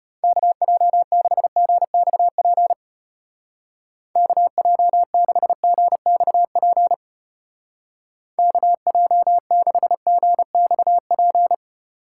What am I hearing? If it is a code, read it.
Morse code, heard as a keyed tone, about 31 wpm, 700 Hz. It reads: KJ6GXP KJ6GXP KJ6GXP
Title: Speed Racing - USA 2x3 Call Signs 25wpm